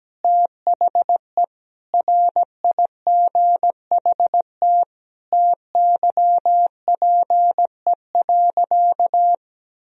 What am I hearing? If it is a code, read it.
Morse code, heard as a keyed tone, about 17 wpm, 700 Hz. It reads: THE RIGHT TYPE.
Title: Sentences from Top 1000 Words - Review Part 3 - 17wpm